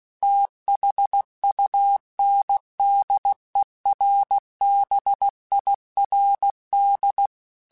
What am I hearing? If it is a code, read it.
Morse code, heard as a keyed tone, about 16 wpm, 800 Hz. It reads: THUNDERBIRD
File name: QSOs / QSOs-mp3s / WD0BGP 1@15